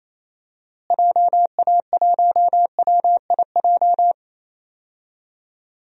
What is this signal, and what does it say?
Morse code, heard as a keyed tone, about 28 wpm, 700 Hz. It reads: JA1WIJ
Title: DXCC-100 Call Signs - 6 Characters - Repeated 2x - 28wpm